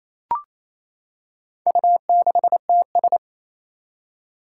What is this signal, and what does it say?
Morse code, heard as a keyed tone, about 27 wpm, 700 Hz. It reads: U6TS
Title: Four Challenging Letter-Number Combinations 28wpm